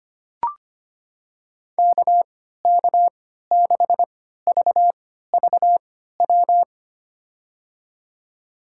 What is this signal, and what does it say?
Morse code, heard as a keyed tone, about 25 wpm, spaced as for 14 wpm, 700 Hz. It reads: KK6VVW
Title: USA Call Signs - 6 Characters - 3x Character Spacing - 25wpm